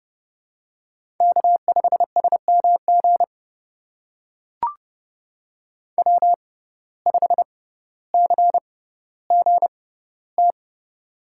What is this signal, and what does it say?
Morse code, heard as a keyed tone, about 30 wpm, spaced as for 15 wpm, 700 Hz. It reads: K5SMG W5CGT
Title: USA Call Signs - 5 Characters - 6x Character Spacing - 30wpm